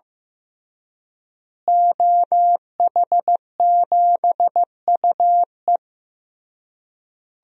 Morse code, heard as a keyed tone, about 15 wpm, 700 Hz. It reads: OH7UE
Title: DXCC-100 Call Signs - 5 Characters - Repeated 3x - 15wpm